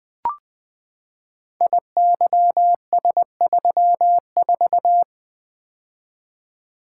Morse code, heard as a keyed tone, about 20 wpm, 700 Hz. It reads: IYS34
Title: Code Groups - 5 Letter:Numbers - 20wpm